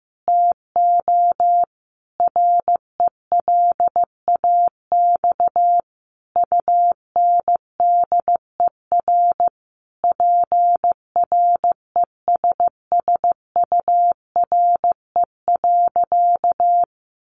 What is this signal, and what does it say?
Morse code, heard as a keyed tone, about 15 wpm, 700 Hz. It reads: TO RELAX UNDER PRESSURE.
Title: Sentences from Top 1400 Words 15wpm